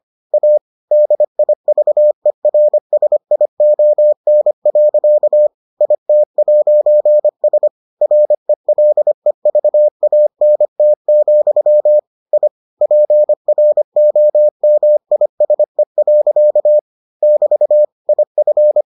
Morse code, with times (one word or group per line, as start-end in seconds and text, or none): 0.3-0.6 A
0.9-5.5 DIVERSION.
5.8-7.7 IT'S
8.0-12.0 RELEVANT,
12.3-12.5 I
12.8-16.8 PROMISE.
17.2-17.8 <BT>
18.1-18.8 IF